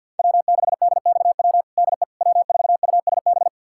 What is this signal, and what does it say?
Morse code, heard as a keyed tone, about 52 wpm, 700 Hz. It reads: W6DXW DE W4USB
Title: USA Callsign DE Callsign - 5 Characters - 50wpm